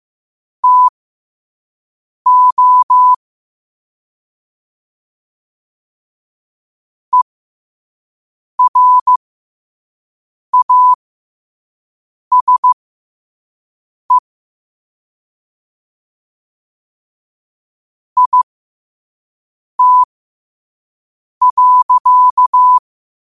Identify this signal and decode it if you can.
Morse code, heard as a keyed tone, about 15 wpm, spaced as for 5 wpm, 1000 Hz. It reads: TO ERASE IT.
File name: qotdfiles / 5wpm